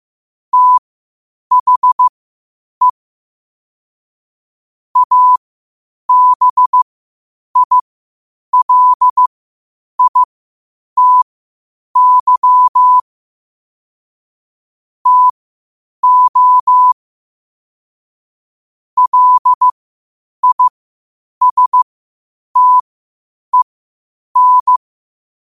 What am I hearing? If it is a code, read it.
Morse code, heard as a keyed tone, about 15 wpm, spaced as for 8 wpm, 1000 Hz. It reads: THE ABILITY TO LISTEN
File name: qotdfiles / 8wpm